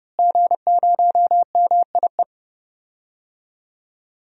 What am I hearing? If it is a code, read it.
Morse code, heard as a keyed tone, about 30 wpm, 700 Hz. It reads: G0MIE
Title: DXCC-100 Call Signs - 5 Characters - Repeated 3x - 30wpm